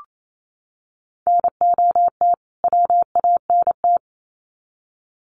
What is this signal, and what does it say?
Morse code, heard as a keyed tone, about 28 wpm, 700 Hz. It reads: NOT WANT
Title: Sets of 2 Words 28wpm